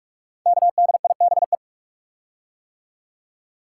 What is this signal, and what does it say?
Morse code, heard as a keyed tone, about 44 wpm, 700 Hz. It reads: KDIBE
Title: Code Groups - 5 Letters - 45wpm